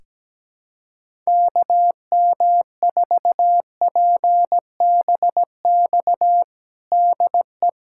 Morse code, heard as a keyed tone, about 17 wpm, 700 Hz. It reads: KM4PBX DE
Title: USA Callsign DE Callsign - 6 Characters - 17wpm